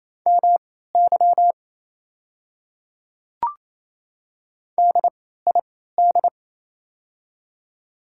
Morse code, heard as a keyed tone, about 28 wpm, spaced as for 16 wpm, 690 Hz. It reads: MY DID
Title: 3x Character Spacing - Top 100 Words - 28wpm